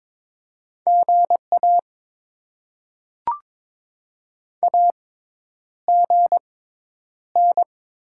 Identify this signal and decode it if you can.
Morse code, heard as a keyed tone, about 22 wpm, spaced as for 10 wpm, 700 Hz. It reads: GA AGN
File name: Top 20 QSO Elements - 6x Character Spacing - 22wpm